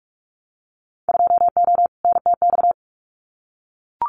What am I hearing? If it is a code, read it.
Morse code, heard as a keyed tone, about 44 wpm, 700 Hz. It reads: 2O NTX
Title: Winter Field Day Exchanges 45wpm